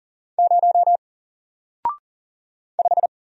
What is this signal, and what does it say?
Morse code, heard as a keyed tone, about 39 wpm, 700 Hz. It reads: 0 5
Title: Single Letter-Number Alt 40wpm